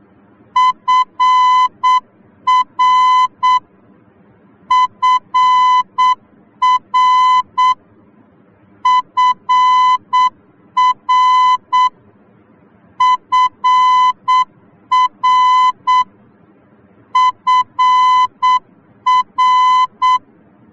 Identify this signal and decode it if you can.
Morse code, heard as a keyed tone, about 8 wpm, 1010 Hz. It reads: FR FR FR FR FR
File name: germany / mx / FR-D